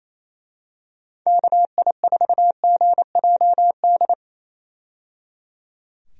Morse code, heard as a keyed tone, about 28 wpm, 700 Hz. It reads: KI4GJD
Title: USA Call Signs - 6 Characters - 28wpm